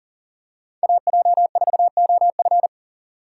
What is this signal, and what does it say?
Morse code, heard as a keyed tone, about 40 wpm, 700 Hz. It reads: AJ4OF
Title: USA Call Signs - 5 Characters - 6x Character Spacing - 40wpm